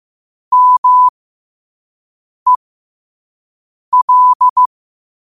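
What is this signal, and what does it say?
Morse code, heard as a keyed tone, about 15 wpm, spaced as for 5 wpm, 1000 Hz. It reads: MEL
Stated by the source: Morse quote of the day at 5 wpm.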